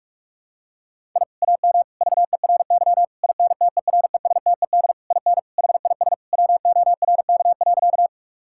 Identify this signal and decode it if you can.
Morse code, heard as a keyed tone, about 43 wpm, 700 Hz. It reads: I AM VERY INTERESTED IN HIS WORK.
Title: Sentences from Top 1000 Words - Review Part 6 - 45wpm